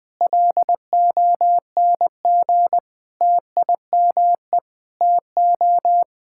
Morse code, heard as a keyed tone, about 20 wpm, 700 Hz. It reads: LONG TIME TO
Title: Sentences with 7-Word Phrases - 20wpm